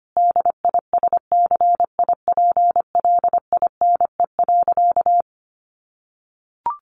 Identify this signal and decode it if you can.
Morse code, heard as a keyed tone, about 25 wpm, 700 Hz. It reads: DISCIPLINE.
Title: Sentences from Top 900 Words 25wpm